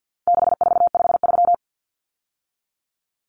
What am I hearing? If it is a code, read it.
Morse code, heard as a keyed tone, about 50 wpm, 700 Hz. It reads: 6453
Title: Numbers - 4-Digit 50wpm